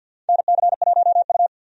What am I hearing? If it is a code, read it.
Morse code, heard as a keyed tone, about 47 wpm, 700 Hz. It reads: NC1U
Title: USA Call Signs - 4 Characters - 50wpm